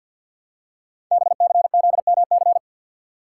Morse code, heard as a keyed tone, about 50 wpm, 690 Hz. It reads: BCZMC
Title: Code Groups - 5 Letters - 50wpm